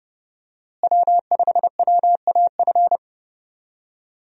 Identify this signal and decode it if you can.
Morse code, heard as a keyed tone, about 32 wpm, 700 Hz. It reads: W5WAF
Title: USA Call Signs - 5 Characters - Repeated 3x - 30wpm